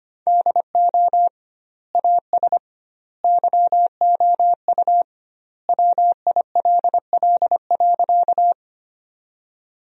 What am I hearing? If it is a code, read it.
Morse code, heard as a keyed tone, about 25 wpm, 700 Hz. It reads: DO AS YOU WILL.
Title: 2x Word-Spacing - Sentences from Top 100 Words - 25wpm